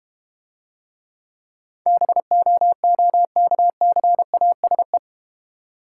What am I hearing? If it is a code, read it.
Morse code, heard as a keyed tone, about 31 wpm, 700 Hz. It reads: BOOKCASE
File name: practice / 25wpm / Difficult Words - Speed-Racing with Hints - 25wpm